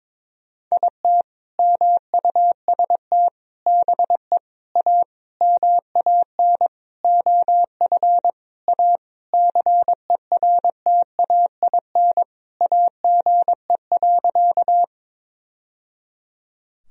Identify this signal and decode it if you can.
Morse code, heard as a keyed tone, about 22 wpm, 700 Hz. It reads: IT MUST BE A MAN OF A CERTAIN AGE.
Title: Sentences from Top 1000 Words - Review Part 3 - 22wpm